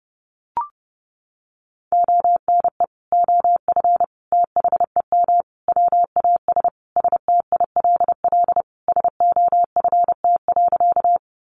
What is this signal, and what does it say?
Morse code, heard as a keyed tone, about 30 wpm, 700 Hz. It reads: ONE OF THEM WAS STILL SOFT.
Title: Sentences from Top 1500 Words 30wpm